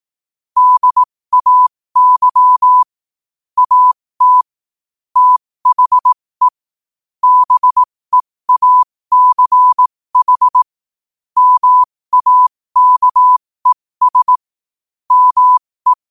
Morse code, heard as a keyed tone, about 18 wpm, spaced as for 15 wpm, 1000 Hz. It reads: DAY AT THE BEACH MAKES ME
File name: qotdfiles / 15wpm